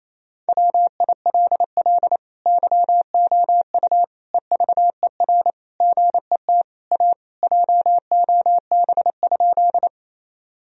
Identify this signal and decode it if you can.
Morse code, heard as a keyed tone, about 28 wpm, 700 Hz. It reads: WILL YOU EVER GET A JOB?
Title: Sentences from Top 1000 Words - Review Part 3 - 28wpm